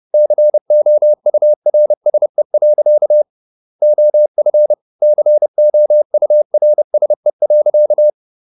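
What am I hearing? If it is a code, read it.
Morse code, heard as a keyed tone, about 30 wpm, 600 Hz. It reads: COURSE. OF COURSE.
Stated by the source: Morse code at 30 wpm, the novel Never by Ken Follett in English, from CW med Gnister.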